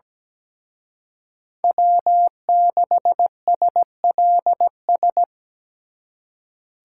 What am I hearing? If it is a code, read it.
Morse code, heard as a keyed tone, about 20 wpm, 700 Hz. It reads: W6SLS
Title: Speed Racing - USA 1x3 Call Signs 17wpm